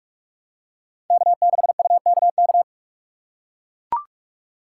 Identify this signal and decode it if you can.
Morse code, heard as a keyed tone, about 44 wpm, 700 Hz. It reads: K6UKK
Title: USA Call Signs - 5 Characters - 6x Character Spacing - 45wpm